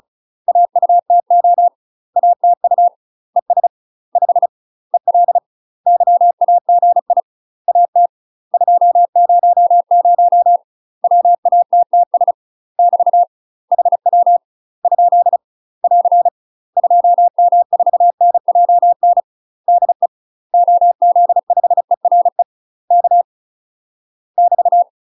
Morse code, heard as a keyed tone, about 34 wpm, 700 Hz. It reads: AUTO ATU ES 5 EL YAGI AT 200 WATTS <BT> HW ? <AR> 2M4NJN DE OZ5ERE K <BT>
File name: UK MASTER QSOs TRAINING / 35WPM / UK MASTER QSOs 0028